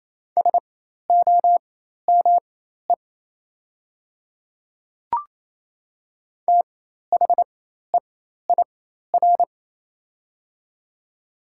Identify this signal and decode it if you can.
Morse code, heard as a keyed tone, about 27 wpm, spaced as for 13 wpm, 690 Hz. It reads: SOME THEIR
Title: 4x Character Spacing - Top 200 Words - 28wpm